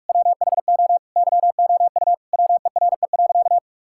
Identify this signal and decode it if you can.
Morse code, heard as a keyed tone, about 46 wpm, 700 Hz. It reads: WHO YOU WERE.
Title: Sentences from Top 1000 Words - Review Part 6 - 45wpm